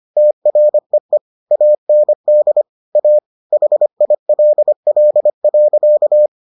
Morse code, heard as a keyed tone, about 25 wpm, 600 Hz. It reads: TREE AND A HILL.